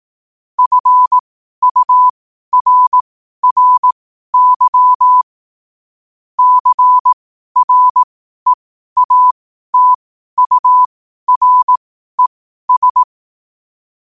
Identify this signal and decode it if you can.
Morse code, heard as a keyed tone, about 18 wpm, spaced as for 12 wpm, 1000 Hz. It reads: FURRY CREATURES
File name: qotdfiles / 12wpm